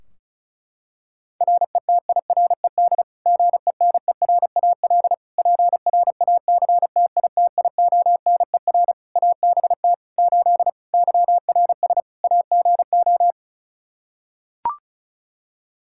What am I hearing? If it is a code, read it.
Morse code, heard as a keyed tone, about 36 wpm, 700 Hz. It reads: RETIRED GENERAL PRACTITIONER ABT 8 YRS AGO
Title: Practice QSO - Exchange 3 - More Info - 35wpm